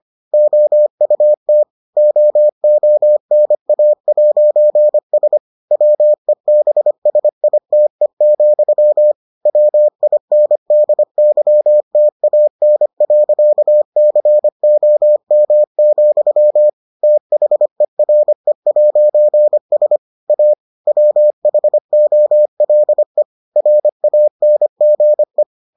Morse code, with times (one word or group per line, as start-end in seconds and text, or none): 0.3-1.6 OUT
2.0-5.4 OONA'S
5.7-9.1 WEBSITE,
9.5-16.7 WINDYTAN.COM,
17.0-20.0 THERE'S
20.3-20.5 A
20.9-23.2 WHOLE
23.6-25.4 RANGE